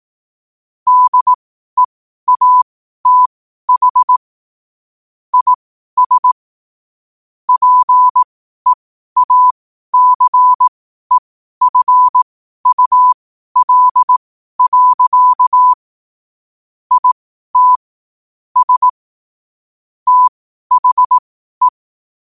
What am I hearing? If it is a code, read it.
Morse code, heard as a keyed tone, about 18 wpm, spaced as for 12 wpm, 1000 Hz. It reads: DEATH IS PEACEFUL. IT S THE
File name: qotdfiles / 12wpm